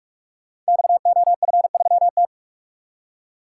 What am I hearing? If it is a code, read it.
Morse code, heard as a keyed tone, about 44 wpm, 700 Hz. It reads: XOF3T
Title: Code Groups - 5 Letter:Numbers - 45wpm